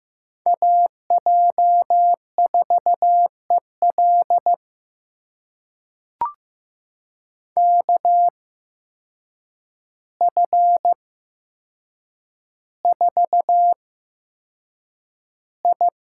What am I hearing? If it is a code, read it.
Morse code, heard as a keyed tone, about 15 wpm, spaced as for 7 wpm, 700 Hz. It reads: AJ4EL KF4I
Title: USA Call Signs - 5 Characters - 8x Character Spacing - 15wpm